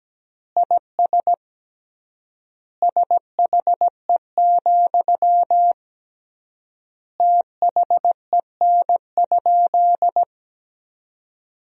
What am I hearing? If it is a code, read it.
Morse code, heard as a keyed tone, about 17 wpm, 700 Hz. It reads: IS SHE, THEN?
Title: 3x Word-Spacing - Sentences from Top 200 Words - 17wpm